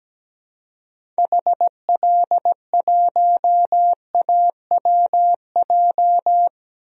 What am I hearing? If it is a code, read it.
Morse code, heard as a keyed tone, about 17 wpm, 700 Hz. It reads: HL1AWJ
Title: DXCC-100 Call Signs - 6 Characters - Repeated 2x - 17wpm